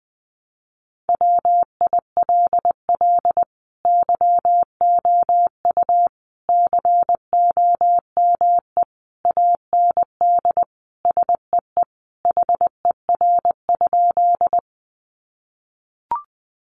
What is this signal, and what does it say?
Morse code, heard as a keyed tone, about 20 wpm, 700 Hz. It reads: WILL YOU COME AND SEE HER?
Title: Sentences from Top 100 Words 20wpm